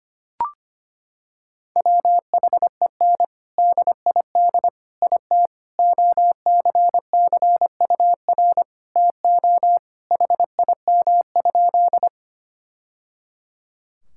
Morse code, heard as a keyed tone, about 25 wpm, 700 Hz. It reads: WHEN DID IT OCCUR TO HIM?
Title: Sentences from Top 1000 Words - Review Part 6 - 25wpm